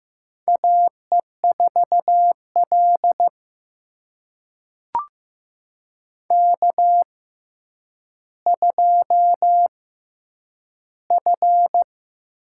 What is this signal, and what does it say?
Morse code, heard as a keyed tone, about 15 wpm, spaced as for 9 wpm, 700 Hz. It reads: AE4L K2F
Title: USA Call Signs - 6x Character Spacing - 15wpm